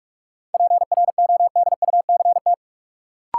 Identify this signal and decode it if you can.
Morse code, heard as a keyed tone, about 46 wpm, 700 Hz. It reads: PRODUCT
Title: Sentences from Top 1400 Words - New Word Drill Alt - 45wpm